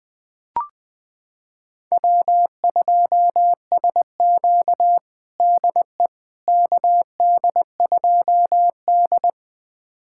Morse code, heard as a keyed tone, about 20 wpm, 700 Hz. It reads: W2SQ DE KD2D
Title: USA Callsign DE Callsign - 4 Characters - 20wpm